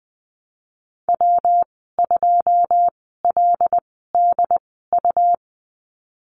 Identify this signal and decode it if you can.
Morse code, heard as a keyed tone, about 20 wpm, 700 Hz. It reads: W 2 L D U
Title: 154 Lesson 25g - Letter p - Call Signs 20wpm